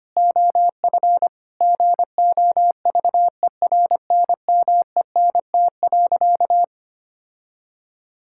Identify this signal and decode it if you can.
Morse code, heard as a keyed tone, about 25 wpm, 700 Hz. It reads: OF GOVERNMENT.
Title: Sentences from Top 1000 Words - Review Part 5 - 25wpm